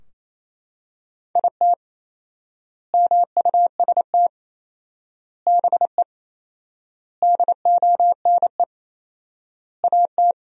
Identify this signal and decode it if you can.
Morse code, heard as a keyed tone, about 28 wpm, 700 Hz. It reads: IT MUST BE DONE AT
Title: 4x Word-Spacing - Sentences from Top 500 Words - 28wpm